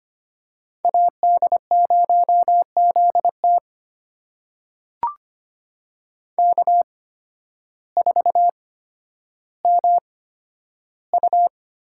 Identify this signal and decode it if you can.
Morse code, heard as a keyed tone, about 25 wpm, spaced as for 12 wpm, 700 Hz. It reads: AD0ZT K4MU
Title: USA Call Signs - 8x Character Spacing - 25wpm